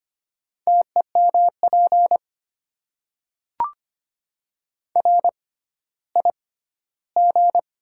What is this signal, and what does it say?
Morse code, heard as a keyed tone, about 25 wpm, spaced as for 14 wpm, 690 Hz. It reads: TEMP RIG
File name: Top 90 QSO Elements - 6x Character Spacing - 25wpm